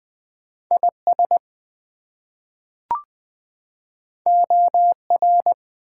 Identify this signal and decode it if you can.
Morse code, heard as a keyed tone, about 20 wpm, 700 Hz. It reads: IS OR
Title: Two Letter Words - 20wpm